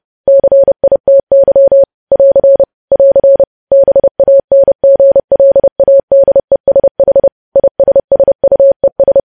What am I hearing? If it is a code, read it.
Morse code, heard as a keyed tone, about 30 wpm, 550 Hz. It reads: CITY <AR> <AR> BANGLADESH ISSUES
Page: news headlines in morse code at 30 wpm